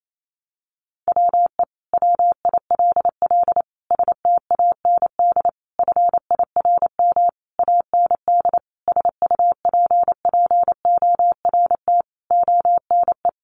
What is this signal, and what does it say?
Morse code, heard as a keyed tone, about 28 wpm, 700 Hz. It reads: WE WILL STAND FIRM AND SUPPORT ONE
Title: Sentences from Top 1400 Words 28wpm